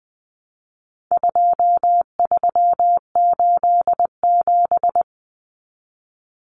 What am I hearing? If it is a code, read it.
Morse code, heard as a keyed tone, about 20 wpm, 700 Hz. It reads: 2387
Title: Common Cut Numbers - 4-Digit 20wpm